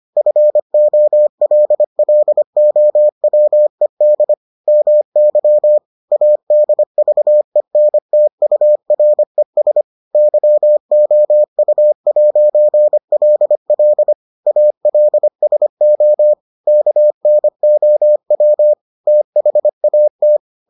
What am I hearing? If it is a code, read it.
Morse code, heard as a keyed tone, about 25 wpm, 600 Hz. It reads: FOLLOWED MY ADVENTURES YOU'LL ALSO KNOW THAT